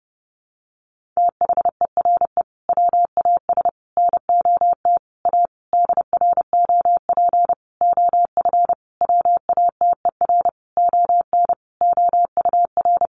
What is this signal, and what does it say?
Morse code, heard as a keyed tone, about 30 wpm, 700 Hz. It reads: THERE WAS NOT A DROP OF WATER ON OUR
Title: Sentences from Top 1400 Words 30wpm